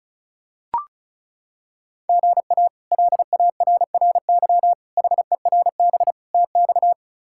Morse code, heard as a keyed tone, about 34 wpm, 700 Hz. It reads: GA LARRY HERB TX
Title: K1USN SST QSO Practice - 35wpm